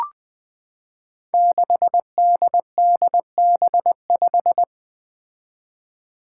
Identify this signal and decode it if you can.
Morse code, heard as a keyed tone, about 20 wpm, 700 Hz. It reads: 6DDB5
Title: Five Challenging Letter-Number Combinations 20wpm